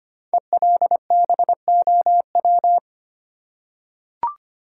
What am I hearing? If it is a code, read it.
Morse code, heard as a keyed tone, about 25 wpm, 700 Hz. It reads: ELBOW
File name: 5 Letter Words - First 3 Letters Given - 25wpm